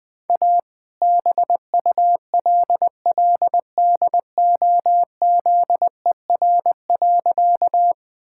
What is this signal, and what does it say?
Morse code, heard as a keyed tone, about 20 wpm, 700 Hz. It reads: A BULLDOZER.
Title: All-Age Appropriate Jokes - 20wpm